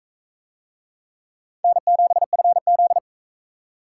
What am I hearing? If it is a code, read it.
Morse code, heard as a keyed tone, about 44 wpm, 700 Hz. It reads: N7FZ